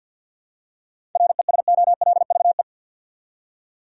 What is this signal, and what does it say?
Morse code, heard as a keyed tone, about 52 wpm, 700 Hz. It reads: RESOLVE
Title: Seven Letter Words - 50wpm